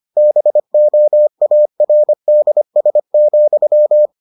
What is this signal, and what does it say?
Morse code, heard as a keyed tone, about 25 wpm, 600 Hz. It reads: BOARDS,